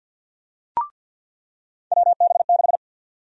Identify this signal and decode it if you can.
Morse code, heard as a keyed tone, about 46 wpm, 700 Hz. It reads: WB6
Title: USA Call Sign Prefixes - 50wpm